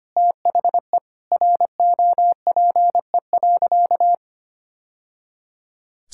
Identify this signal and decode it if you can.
Morse code, heard as a keyed tone, about 25 wpm, 700 Hz. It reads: THE ROPE.
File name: Easy Sentences with Hints - 25wpm